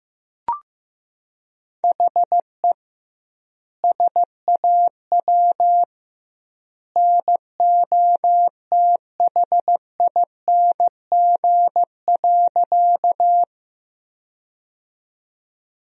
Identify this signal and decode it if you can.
Morse code, heard as a keyed tone, about 15 wpm, 700 Hz. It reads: HE SAW NOTHING.
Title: 2x Word-Spacing - Sentences from Top 500 Words - 15wpm